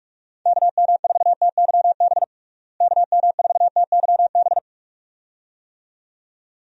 Morse code, heard as a keyed tone, about 46 wpm, 700 Hz. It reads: KM4TYB KM4TYB
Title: USA Call Signs - 6 Characters - Repeated 2x - 45wpm